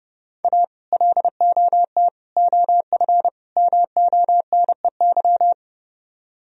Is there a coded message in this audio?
Yes: Morse code from 0.4 to 5.5 s, heard as a keyed tone, about 30 wpm, 700 Hz. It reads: A LOT OF MONEY